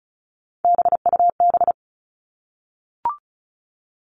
Morse code, heard as a keyed tone, about 34 wpm, 700 Hz. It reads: BUB
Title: All Cut Numbers - 3-Digit 35wpm